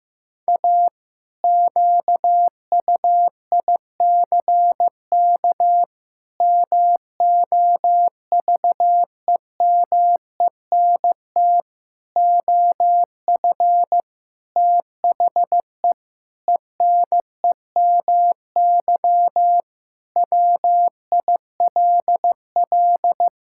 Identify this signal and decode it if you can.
Morse code, heard as a keyed tone, about 16 wpm, 700 Hz. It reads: A QUICK MOVEMENT OF THE ENEMY WILL
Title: Pangram Challenge v2 - 15wpm